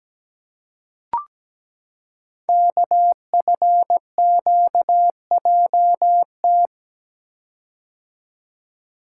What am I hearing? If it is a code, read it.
Morse code, heard as a keyed tone, about 17 wpm, 700 Hz. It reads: KFQJT
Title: Code Groups - 5 Letters - 17wpm